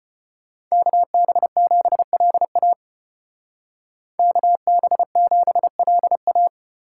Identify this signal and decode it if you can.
Morse code, heard as a keyed tone, about 31 wpm, 700 Hz. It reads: KB7LA KB7LA
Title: Speed Racing - USA 2x2 Call Signs 25wpm